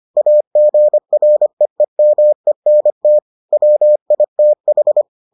Morse code, heard as a keyed tone, about 25 wpm, 600 Hz. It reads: AGREEMENT WITH